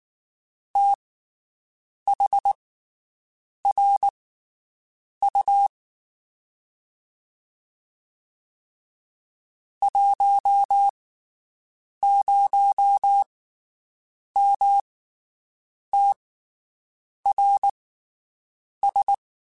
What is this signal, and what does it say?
Morse code, heard as a keyed tone, about 19 wpm, spaced as for 7 wpm, 800 Hz. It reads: THRU 10MTRS
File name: QSOs Spaced